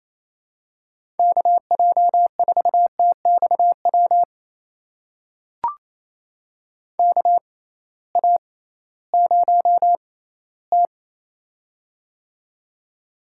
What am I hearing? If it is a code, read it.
Morse code, heard as a keyed tone, about 28 wpm, spaced as for 16 wpm, 700 Hz. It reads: KJ4TXW KA0T
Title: USA Call Signs - 6x Character Spacing - 28wpm